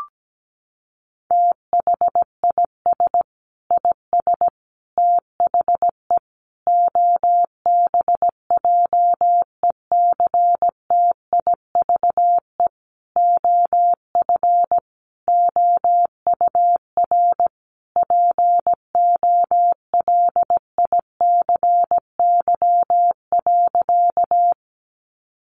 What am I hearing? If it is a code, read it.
Morse code, heard as a keyed tone, about 17 wpm, 700 Hz. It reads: THIS IS THE OBJECTIVE OF OUR POLICY.
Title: Sentences from Top 1200 Words 17wpm